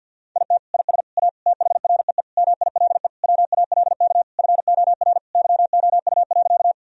Morse code, heard as a keyed tone, about 52 wpm, 700 Hz. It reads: IT IS A THREE MILE WALK FOR YOU.
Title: Sentences from Top 1000 Words - Review Part 1 - 50wpm